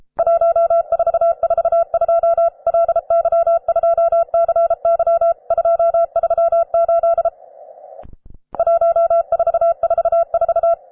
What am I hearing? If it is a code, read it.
Morse code, heard as a keyed tone, about 33 wpm, 670 Hz. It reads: 1442 LY2CY 238 1444
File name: wae / cw advanced / rw9sw cw